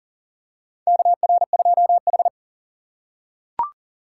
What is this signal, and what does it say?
Morse code, heard as a keyed tone, about 40 wpm, 700 Hz. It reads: KR2H
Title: USA Call Signs - 4x Character Spacing - 40wpm